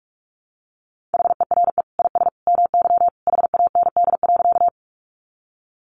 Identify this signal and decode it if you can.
Morse code, heard as a keyed tone, about 44 wpm, 700 Hz. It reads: HERE IS MY HAND.
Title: Sentences from Top 200 Words 45wpm